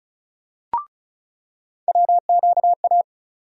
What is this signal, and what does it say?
Morse code, heard as a keyed tone, about 36 wpm, 700 Hz. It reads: WQA
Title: USA Call Sign Postfixes - 3 Letters 35wpm